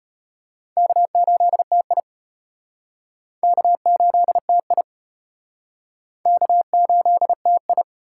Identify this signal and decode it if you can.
Morse code, heard as a keyed tone, about 32 wpm, 700 Hz. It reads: K8TI K8TI K8TI